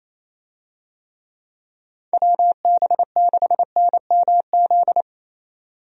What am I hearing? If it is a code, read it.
Morse code, heard as a keyed tone, about 28 wpm, 700 Hz. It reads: WB6NMZ
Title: USA Call Signs - 6 Characters - 28wpm